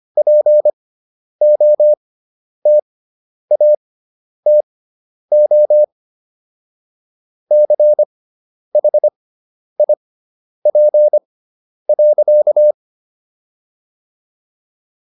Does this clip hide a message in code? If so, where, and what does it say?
Yes: Morse code from 0.2 to 12.7 s, heard as a keyed tone, about 25 wpm, spaced as for 10 wpm, 600 Hz. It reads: POTATO CHIP.